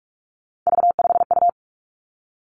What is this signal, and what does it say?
Morse code, heard as a keyed tone, about 46 wpm, 700 Hz. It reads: V5U
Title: Three Challenging Letter-Number Combinations 45wpm